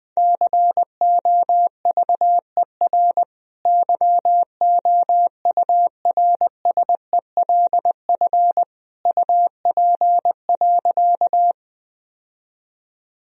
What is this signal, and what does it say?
Morse code, heard as a keyed tone, about 21 wpm, 700 Hz. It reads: COVER YOURSELF UP.
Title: Sentences from Top 1000 Words - Review Part 1 - 20wpm